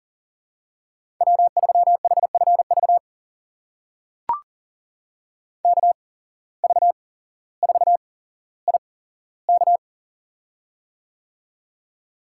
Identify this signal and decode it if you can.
Morse code, heard as a keyed tone, about 37 wpm, spaced as for 17 wpm, 700 Hz. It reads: W3HFV KV4IK
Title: USA Call Signs - 5 Characters - 8x Character Spacing - 40wpm